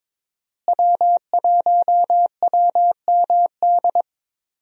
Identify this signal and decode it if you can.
Morse code, heard as a keyed tone, about 22 wpm, 700 Hz. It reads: W1WMD
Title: USA Call Signs - 5 Characters - Repeated 3x - 22wpm